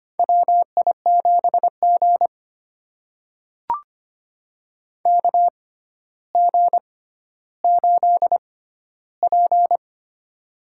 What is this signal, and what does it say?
Morse code, heard as a keyed tone, about 25 wpm, spaced as for 13 wpm, 700 Hz. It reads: WI7G KG8P
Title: USA Call Signs - 6x Character Spacing - 25wpm